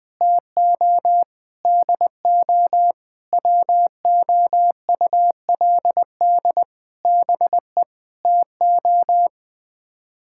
Morse code, heard as a keyed tone, about 20 wpm, 700 Hz. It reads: TO DO WOULD BE TO
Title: Sets of 8 Words - 20wpm